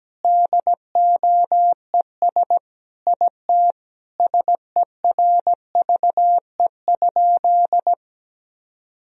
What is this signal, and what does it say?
Morse code, heard as a keyed tone, about 17 wpm, 700 Hz. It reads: DOES IT SERVE?
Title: Sentences from Top 1000 Words - Review Part 2 - 17wpm